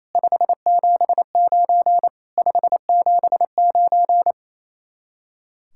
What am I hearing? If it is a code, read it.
Morse code, heard as a keyed tone, about 35 wpm, 700 Hz. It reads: 579 579
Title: Speed-Racing Common QSO Phrases - 28wpm